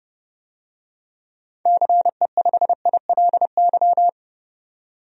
Morse code, heard as a keyed tone, about 29 wpm, 700 Hz. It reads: CE5ILY
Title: DXCC-100 Call Signs - 6 Characters - 30wpm